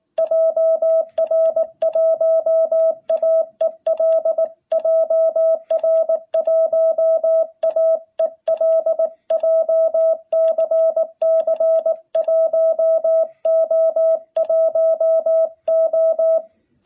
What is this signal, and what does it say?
Morse code, heard as a keyed tone, about 19 wpm, 660 Hz. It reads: JR1AEL JR1AEL JCC 1O1O
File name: jh8cbh / 2021cw